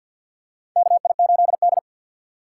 Morse code, heard as a keyed tone, about 52 wpm, 700 Hz. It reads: KI8D
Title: USA Call Signs - 6x Character Spacing - 50wpm